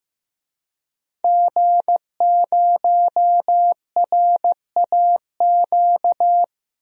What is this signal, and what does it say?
Morse code, heard as a keyed tone, about 15 wpm, 700 Hz. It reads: G0RAQ
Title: DXCC-100 Call Signs - 5 Characters - Repeated 2x - 15wpm